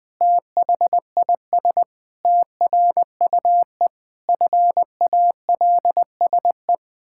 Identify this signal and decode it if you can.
Morse code, heard as a keyed tone, about 20 wpm, 700 Hz. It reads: THIS TRUE FALSE
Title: Pangram Challenge v2 - 20wpm